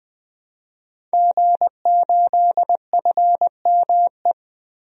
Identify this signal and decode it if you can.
Morse code, heard as a keyed tone, about 21 wpm, 700 Hz. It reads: G8FME
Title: DXCC-100 Call Signs - 5 Characters - 20wpm